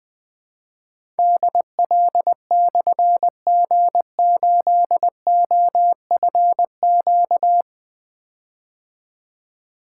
Morse code, heard as a keyed tone, about 21 wpm, 700 Hz. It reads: DL/G8OFQ
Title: Difficult Call Signs 20wpm